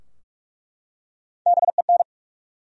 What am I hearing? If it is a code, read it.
Morse code, heard as a keyed tone, about 46 wpm, 700 Hz. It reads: BEN
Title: USA Common Names - Max 4 Letters - 45wpm